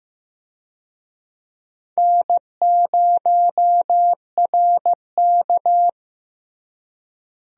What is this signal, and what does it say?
Morse code, heard as a keyed tone, about 15 wpm, 700 Hz. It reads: N0RK
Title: USA Call Signs - 4 Characters - Repeated 2x - 15wpm